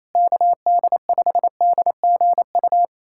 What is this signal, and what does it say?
Morse code, heard as a keyed tone, about 28 wpm, 700 Hz. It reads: KD5DGU
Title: USA Callsign DE Callsign 28wpm